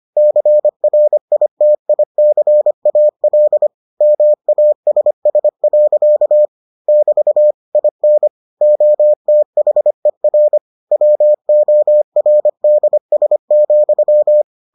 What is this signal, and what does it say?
Morse code, heard as a keyed tone, about 25 wpm, 600 Hz. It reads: CRITICAL MASS. <BT> IN OTHER WORDS,